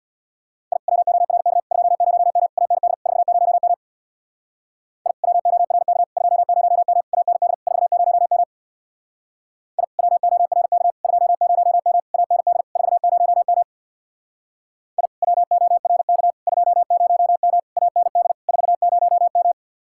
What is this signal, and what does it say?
Morse code, heard as a keyed tone, about 62 wpm, 700 Hz. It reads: I WORK 20M AND 40M I WORK 20M AND 40M I WORK 20M AND 40M I WORK 20M AND 40M
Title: Speed-Racing Common QSO Phrases - 50wpm